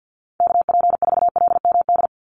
Morse code, heard as a keyed tone, about 50 wpm, 700 Hz. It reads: KP4LMD
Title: USA Call Signs 50wpm